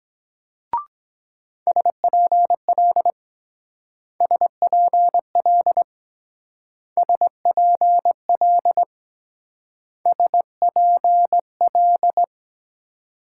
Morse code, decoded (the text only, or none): SPL SPL SPL SPL